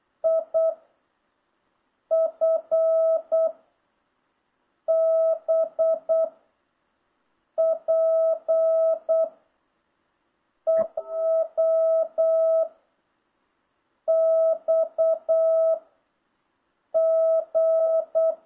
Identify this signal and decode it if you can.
Morse code, heard as a keyed tone, about 8 wpm, 660 Hz. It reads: I F B P EW X G